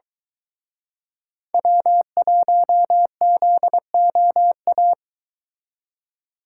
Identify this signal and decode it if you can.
Morse code, heard as a keyed tone, about 23 wpm, 700 Hz. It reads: W1ZOA